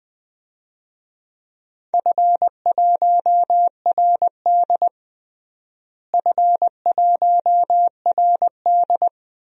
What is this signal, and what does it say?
Morse code, heard as a keyed tone, about 20 wpm, 700 Hz. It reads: F1RD F1RD
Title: DXCC-100 Call Signs - 4 Characters - Repeated 2x - 20wpm